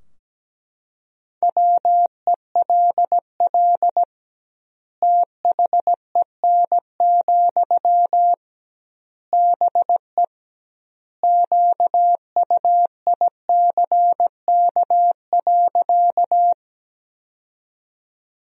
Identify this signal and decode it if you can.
Morse code, heard as a keyed tone, about 17 wpm, 700 Hz. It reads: WELL THEN, BE QUICK.
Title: 2x Word-Spacing - Sentences from Top 500 Words - 17wpm